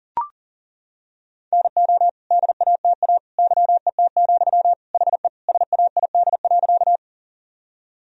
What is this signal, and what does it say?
Morse code, heard as a keyed tone, about 40 wpm, 700 Hz. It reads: NO DATA YET, HE SAID.
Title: Sentences from Top 1000 Words - Review Part 3 - 40wpm